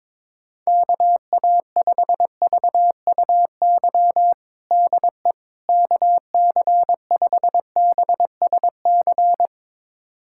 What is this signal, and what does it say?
Morse code, heard as a keyed tone, about 22 wpm, 700 Hz. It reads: KA5VUY DE KC5BSC
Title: USA Callsign DE Callsign - 6 Characters - 22wpm